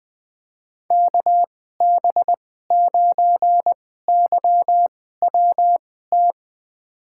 Morse code, heard as a keyed tone, about 20 wpm, spaced as for 15 wpm, 700 Hz. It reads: KB9YWT
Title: USA Call Signs - 2x Character Spacing - 20wpm